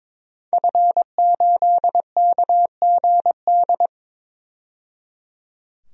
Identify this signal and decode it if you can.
Morse code, heard as a keyed tone, about 22 wpm, 700 Hz. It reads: F8KGD
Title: DXCC-100 Call Signs - 5 Characters - 22wpm